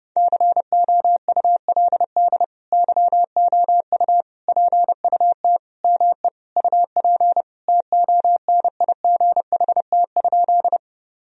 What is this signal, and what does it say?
Morse code, heard as a keyed tone, about 30 wpm, 700 Hz. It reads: COULD YOU PUT ME UP TONIGHT?
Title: Sentences from Top 1000 Words - Review Part 3 - 30wpm